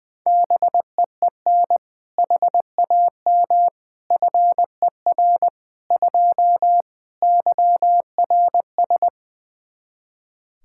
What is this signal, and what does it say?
Morse code, heard as a keyed tone, about 20 wpm, 700 Hz. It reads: BEEN HAM FER 2 YRS
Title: Practice QSO - Exchange 3 - More Info - 20wpm